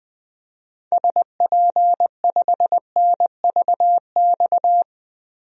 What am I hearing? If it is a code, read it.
Morse code, heard as a keyed tone, about 20 wpm, 700 Hz. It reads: SP5NVX
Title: DXCC-100 Call Signs - 6 Characters - 20wpm